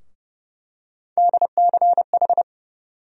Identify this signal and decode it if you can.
Morse code, heard as a keyed tone, about 29 wpm, 700 Hz. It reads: DCH
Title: USA Call Sign Postfixes - 3 Letters 30wpm